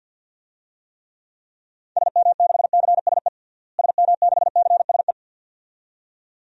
Morse code, heard as a keyed tone, about 50 wpm, 700 Hz. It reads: SM6CSE SM6CSE
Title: DXCC-100 Call Signs - 6 Characters - Repeated 2x - 50wpm